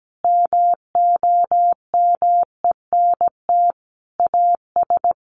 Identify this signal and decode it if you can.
Morse code, heard as a keyed tone, about 17 wpm, 700 Hz. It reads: MOMENT AS
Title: Sentences from Top 1000 Words 17wpm